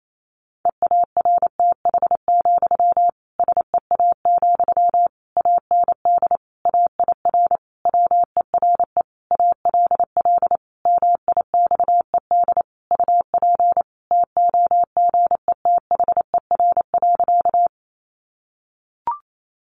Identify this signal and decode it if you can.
Morse code, heard as a keyed tone, about 28 wpm, 700 Hz. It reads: EARTH, SEA, AND AIR WERE ALL MIXED UP TOGETHER.
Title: Sentences from Top 1500 Words 28wpm